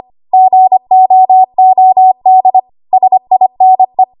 Morse code, heard as a keyed tone, about 25 wpm, 750 Hz. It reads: GOOD SINE
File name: Archive / 25 WPM / 25WPM